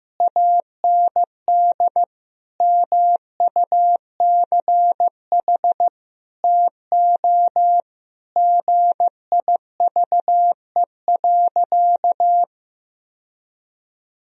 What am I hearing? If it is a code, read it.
Morse code, heard as a keyed tone, about 15 wpm, 700 Hz. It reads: AND MUCH TO GIVE.
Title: Sentences from Top 1000 Words - Review Part 4 - 15wpm